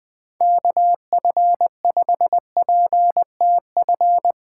Morse code, heard as a keyed tone, about 21 wpm, 700 Hz. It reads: KF5PTF